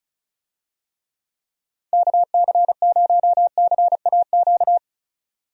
Morse code, heard as a keyed tone, about 34 wpm, 700 Hz. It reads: KC0CAQ
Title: USA Call Signs - 6 Characters - 35wpm